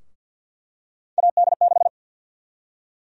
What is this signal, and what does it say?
Morse code, heard as a keyed tone, about 50 wpm, 700 Hz. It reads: AD6
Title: USA Call Sign Prefixes - Repeated 2x - 50wpm